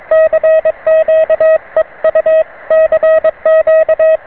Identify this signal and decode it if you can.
Morse code, heard as a keyed tone, about 22 wpm, 630 Hz. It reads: CQ EU CQ